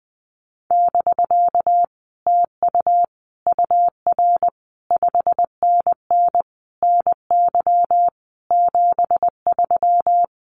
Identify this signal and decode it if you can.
Morse code, heard as a keyed tone, about 20 wpm, 700 Hz. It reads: <BK> TU UR 5NN NY 73
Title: Common QSO Phrases Encore 20wpm